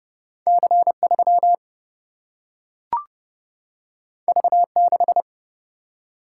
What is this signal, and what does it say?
Morse code, heard as a keyed tone, about 31 wpm, 700 Hz. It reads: C 3 V 6
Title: Single-Letter Single-Number 30wpm